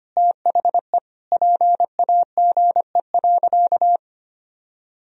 Morse code, heard as a keyed tone, about 25 wpm, 700 Hz. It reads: THE PAGE.
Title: Sentences from Top 1000 Words - Review Part 1 - 25wpm